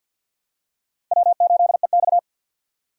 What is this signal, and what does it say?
Morse code, heard as a keyed tone, about 47 wpm, 700 Hz. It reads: W8EX
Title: USA Call Signs - 4 Characters - 2x Character Spacing - 50wpm